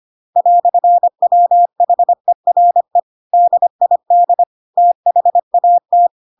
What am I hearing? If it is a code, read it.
Morse code, heard as a keyed tone, about 25 wpm, 700 Hz. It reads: "WHERE DID THAT
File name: Pinocchio / Pinocchio ch1 25 25 ik1hge